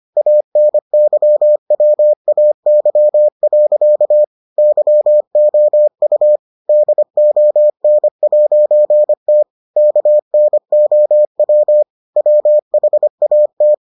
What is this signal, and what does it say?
Morse code, heard as a keyed tone, about 25 wpm, 600 Hz. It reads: ANYWAY. YOU DON'T KNOW WHAT